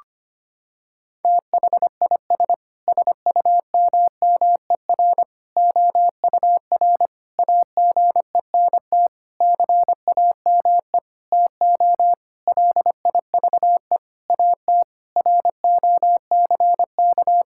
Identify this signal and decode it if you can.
Morse code, heard as a keyed tone, about 25 wpm, 700 Hz. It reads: THIS SUMMER OUR AGENT CAME TO LIVE AT ROCK
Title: Sentences from Top 1600 Words 25wpm